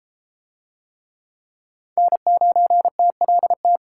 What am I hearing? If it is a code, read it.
Morse code, heard as a keyed tone, about 31 wpm, 700 Hz. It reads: N9TLT